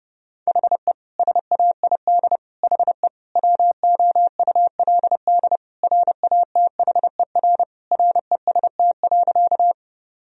Morse code, heard as a keyed tone, about 30 wpm, 700 Hz. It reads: HE SAID HE WOULD RATHER REST.
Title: Sentences from Top 1000 Words - Review Part 4 - 30wpm